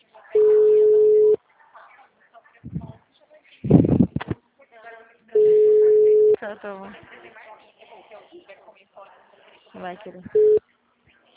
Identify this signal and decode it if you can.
Ringback tone (Europe, 425 Hz), with speech.
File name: out- -2018- - - .28685